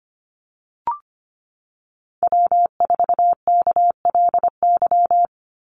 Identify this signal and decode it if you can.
Morse code, heard as a keyed tone, about 25 wpm, 700 Hz. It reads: W4KLY
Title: Call Signs - 25wpm